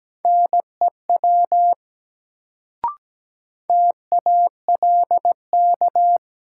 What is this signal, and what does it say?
Morse code, heard as a keyed tone, about 17 wpm, 700 Hz. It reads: NEW TALK
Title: Sets of 2 Words - New Word Drill - 17wpm